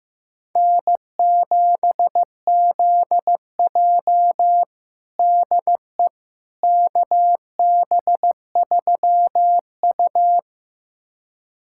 Morse code, heard as a keyed tone, about 15 wpm, 700 Hz. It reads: N7ZJ DE KB3U
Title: USA Callsign DE Callsign - 4 Characters - 15wpm